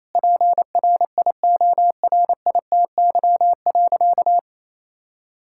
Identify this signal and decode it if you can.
Morse code, heard as a keyed tone, about 28 wpm, 700 Hz. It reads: PRIORITY.
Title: Sentences from Top 1600 Words 28wpm